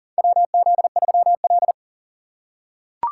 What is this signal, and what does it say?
Morse code, heard as a keyed tone, about 38 wpm, 700 Hz. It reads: WZ3L
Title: USA Call Signs - 3x Character Spacing - 40wpm